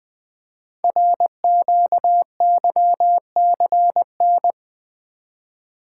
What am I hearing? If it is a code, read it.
Morse code, heard as a keyed tone, about 20 wpm, 700 Hz. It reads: RQYCN
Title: Code Groups - 5 Letters - 20wpm